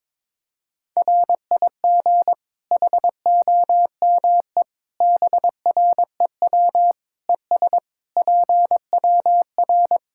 Morse code, heard as a keyed tone, about 22 wpm, 700 Hz. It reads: RIG HOME BREW ES PWR
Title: Practice QSO - Exchange 2 - Info - 22wpm